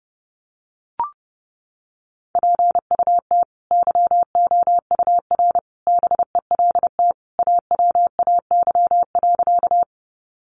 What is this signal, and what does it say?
Morse code, heard as a keyed tone, about 30 wpm, 700 Hz. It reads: PUT YOUR BELT AWAY.
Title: Sentences from Top 1400 Words 30wpm